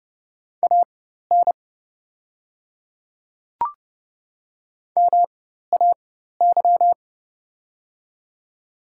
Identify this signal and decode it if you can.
Morse code, heard as a keyed tone, about 30 wpm, spaced as for 14 wpm, 700 Hz. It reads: AN MAY
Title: 4x Character Spacing - Top 100 Words - 30wpm